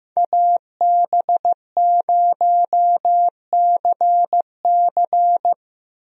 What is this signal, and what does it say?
Morse code, heard as a keyed tone, about 16 wpm, 700 Hz. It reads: AB0CC
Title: USA Call Signs - 5 Characters - Repeated 2x - 15wpm